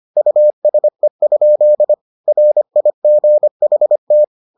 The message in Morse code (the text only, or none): USE? RIGHT